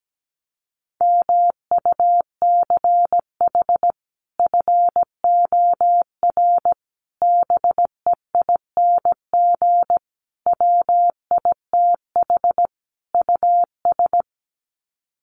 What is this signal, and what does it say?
Morse code, heard as a keyed tone, about 17 wpm, 700 Hz. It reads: MUCH FOR BEING WITH US
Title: Sets of 5 Words 17wpm